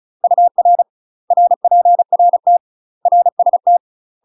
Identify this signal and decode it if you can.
Morse code, heard as a keyed tone, about 35 wpm, 700 Hz. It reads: UR RPRT RST
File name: UK MASTER QSOs TRAINING / 35WPM / UK MASTER QSOs 0000